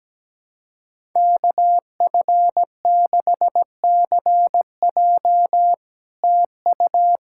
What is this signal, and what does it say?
Morse code, heard as a keyed tone, about 17 wpm, 700 Hz. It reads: KF6CJ TU
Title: SOTA QSO Practice 17wpm